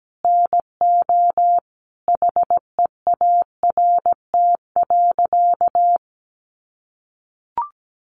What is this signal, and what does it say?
Morse code, heard as a keyed tone, about 17 wpm, 700 Hz. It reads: NO HEART.
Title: Sentences from Top 1200 Words 17wpm